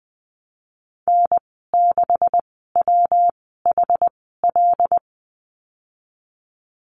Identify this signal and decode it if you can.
Morse code, heard as a keyed tone, about 20 wpm, spaced as for 15 wpm, 700 Hz. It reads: N6WHL
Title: 128 Lesson 21g - Number 6 - Call Signs 20wpm